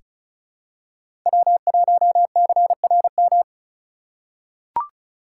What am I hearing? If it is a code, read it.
Morse code, heard as a keyed tone, about 36 wpm, 700 Hz. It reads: W1CRM
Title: USA Call Signs - 5 Characters - 4x Character Spacing - 35wpm